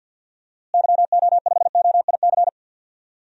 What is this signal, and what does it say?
Morse code, heard as a keyed tone, about 46 wpm, 700 Hz. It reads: YO5OIC
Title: DXCC-100 Call Signs - 6 Characters - Repeated 2x - 50wpm